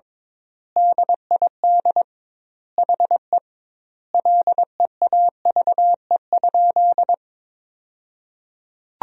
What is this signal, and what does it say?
Morse code, heard as a keyed tone, about 22 wpm, 700 Hz. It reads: DID HE LEAVE?
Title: 2x Word-Spacing - Sentences from Top 400 Words - 22wpm